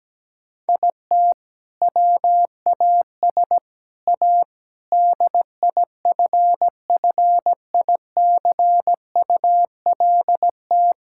Morse code, heard as a keyed tone, about 17 wpm, 700 Hz. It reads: IT WAS A DIFFICULT
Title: Sentences from Top 1000 Words - Review Part 6 - 17wpm